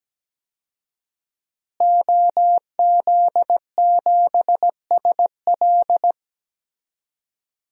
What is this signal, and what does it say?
Morse code, heard as a keyed tone, about 17 wpm, 700 Hz. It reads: OZ7SL
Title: DXCC-100 Call Signs - 5 Characters - Repeated 2x - 17wpm